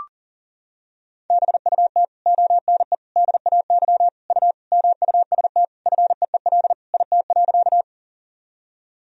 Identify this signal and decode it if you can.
Morse code, heard as a keyed tone, about 40 wpm, 700 Hz. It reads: BUT ONE DAY U MUST FEEL IT.
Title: Sentences from Top 400 Words and Abbreviations 40wpm